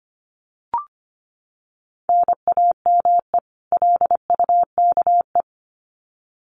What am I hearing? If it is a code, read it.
Morse code, heard as a keyed tone, about 25 wpm, 700 Hz. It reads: NAME LUKE
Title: Common QSO Phrases 25wpm